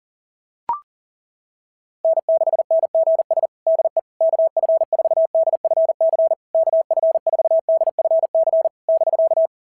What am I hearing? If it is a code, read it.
Morse code, heard as a keyed tone, about 37 wpm, 660 Hz. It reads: N6NGS DE KF4DFC KF4DFC <BK>
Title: SOTA QSO Practice 40wpm